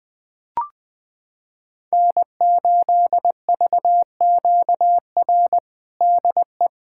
Morse code, heard as a keyed tone, about 20 wpm, 700 Hz. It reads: N8VQR DE
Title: USA Callsign DE Callsign - 5 Characters - 20wpm